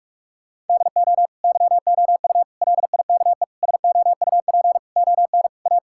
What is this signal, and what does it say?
Morse code, heard as a keyed tone, about 42 wpm, 700 Hz. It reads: DO YOU LIKE SOUP ON A
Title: Sentences from Top 1000 Words - Review Part 6 - 45wpm